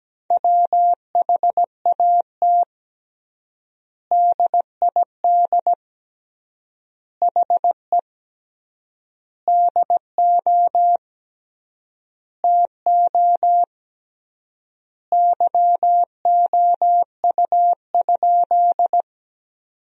Morse code, heard as a keyed tone, about 17 wpm, 700 Hz. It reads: WHAT DID HE DO TO YOU?
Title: 3x Word-Spacing - Sentences from Top 100 Words - 17wpm